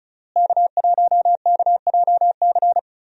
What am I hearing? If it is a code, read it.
Morse code, heard as a keyed tone, about 36 wpm, 700 Hz. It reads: K1KJC
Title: USA Call Signs - 5 Characters - Repeated 3x - 35wpm